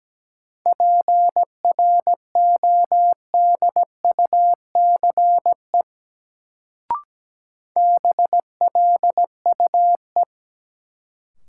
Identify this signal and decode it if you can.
Morse code, heard as a keyed tone, about 17 wpm, 700 Hz. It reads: PRODUCE BLUE
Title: Sentences from Top 1000 Words - Review Part 3 - New Word Drill - 17wpm